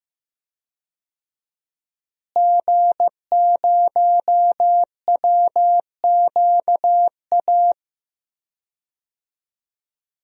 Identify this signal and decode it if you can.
Morse code, heard as a keyed tone, about 15 wpm, 700 Hz. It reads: G0WQA
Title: DXCC-100 Call Signs - 5 Characters - Repeated 2x - 15wpm